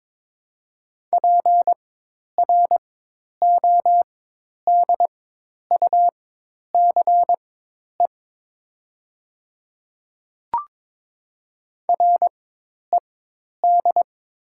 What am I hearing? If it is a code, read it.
Morse code, heard as a keyed tone, about 22 wpm, spaced as for 10 wpm, 700 Hz. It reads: PRODUCE RED
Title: 4x Character Spacing - Top 500 Words - 22wpm